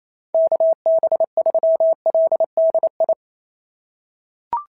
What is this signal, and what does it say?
Morse code, heard as a keyed tone, about 27 wpm, 690 Hz. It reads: KB3LDI
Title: SOTA QSO Practice 28wpm